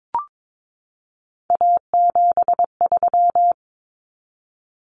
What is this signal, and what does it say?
Morse code, heard as a keyed tone, about 22 wpm, 700 Hz. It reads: A73
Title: Common Cut Numbers - 3-Digit 22wpm